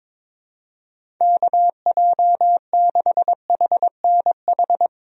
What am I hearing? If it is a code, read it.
Morse code, heard as a keyed tone, about 22 wpm, 700 Hz. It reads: KJ6HNH
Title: USA Call Signs - 6 Characters - Repeated 3x - 22wpm